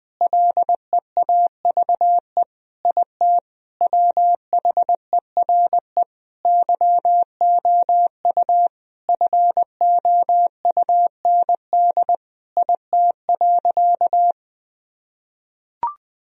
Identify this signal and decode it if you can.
Morse code, heard as a keyed tone, about 20 wpm, 700 Hz. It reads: LEAVE IT WHERE YOU FOUND IT.
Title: Sentences from Top 1000 Words - Review Part 4 - 20wpm